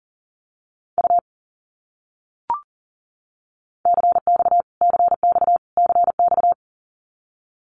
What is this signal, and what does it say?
Morse code, heard as a keyed tone, about 40 wpm, 700 Hz. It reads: U CX CX CX
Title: Abbreviations Repeated 3x - 40wpm